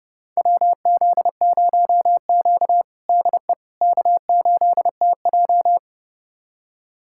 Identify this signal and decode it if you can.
Morse code, heard as a keyed tone, about 32 wpm, 700 Hz. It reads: WZ0Q DE K8TJ
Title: USA Callsign DE Callsign - 4 Characters - 30wpm